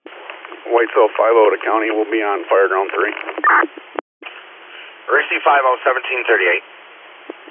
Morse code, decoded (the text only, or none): none